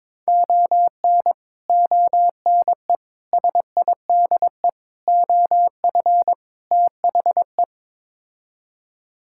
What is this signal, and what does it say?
Morse code, heard as a keyed tone, about 21 wpm, 700 Hz. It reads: ON ONE SIDE OF THE
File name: practice / 22wpm / Build A Phrase - 5-Word Phrases - 22wpm